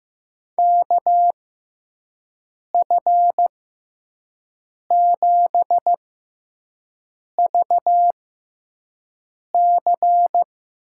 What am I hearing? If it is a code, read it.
Morse code, heard as a keyed tone, about 15 wpm, spaced as for 5 wpm, 700 Hz. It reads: KF7VC
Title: USA Call Signs - 6 Characters - 6x Character Spacing - 15wpm